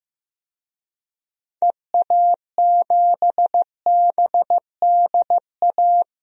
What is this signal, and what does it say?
Morse code, heard as a keyed tone, about 16 wpm, 700 Hz. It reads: EA7BDA
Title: DXCC-100 Call Signs - 6 Characters - Repeated 2x - 15wpm